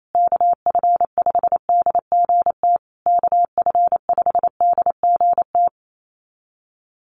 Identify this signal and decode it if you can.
Morse code, heard as a keyed tone, about 28 wpm, 700 Hz. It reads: KF5DGT KF5DGT
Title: USA Call Signs Repeated 3x 28wpm